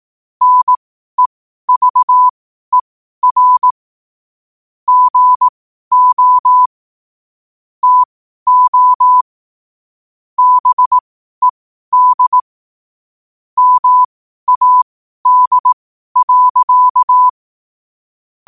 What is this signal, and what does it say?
Morse code, heard as a keyed tone, about 18 wpm, spaced as for 12 wpm, 1000 Hz. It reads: NEVER GO TO BED MAD.
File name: qotdfiles / 12wpm